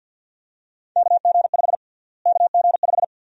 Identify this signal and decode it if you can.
Morse code, heard as a keyed tone, about 47 wpm, 700 Hz. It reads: KG5 KG5
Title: USA Call Sign Prefixes - Repeated 2x - 50wpm